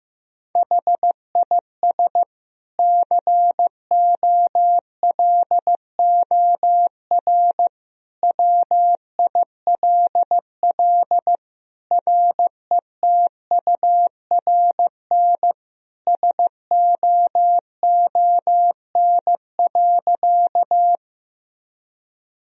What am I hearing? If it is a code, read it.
Morse code, heard as a keyed tone, about 16 wpm, 700 Hz. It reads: HIS COLOR WILL RETURN SOON.
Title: Sentences from Top 1000 Words - Review Part 2 - 15wpm